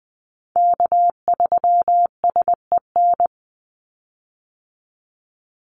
Morse code, heard as a keyed tone, about 20 wpm, 700 Hz. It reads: K3SEN
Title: Call Signs Repeated 2x - 20wpm